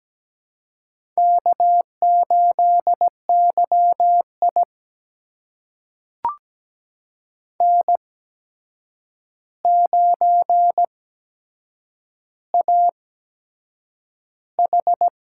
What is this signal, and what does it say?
Morse code, heard as a keyed tone, about 17 wpm, spaced as for 7 wpm, 700 Hz. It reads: K8YI N9AH
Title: USA Call Signs - 4 Characters - 8x Character Spacing - 17wpm